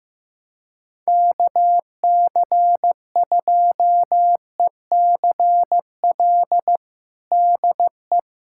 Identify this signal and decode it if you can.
Morse code, heard as a keyed tone, about 15 wpm, 700 Hz. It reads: KC2ECL DE
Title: USA Callsign DE Callsign - 6 Characters - 15wpm